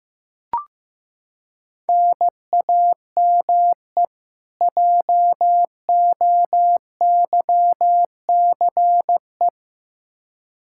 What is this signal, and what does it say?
Morse code, heard as a keyed tone, about 15 wpm, 700 Hz. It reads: NAME JOYCE
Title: K1USN SST QSO Practice 15wpm